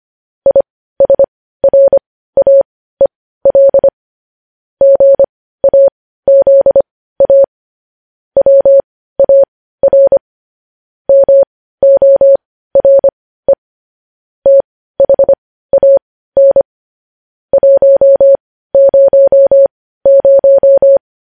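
Morse code, heard as a keyed tone, about 25 wpm, spaced as for 15 wpm, 550 Hz. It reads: ISRAEL GAZA WAR MORE THAN 100
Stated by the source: Morse news headlines practice at 15 wpm.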